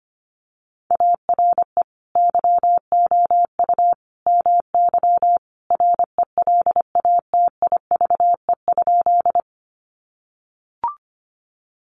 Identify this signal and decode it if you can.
Morse code, heard as a keyed tone, about 25 wpm, 700 Hz. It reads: ARE YOU MY RELATIVE?
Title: Sentences from Top 1400 Words 25wpm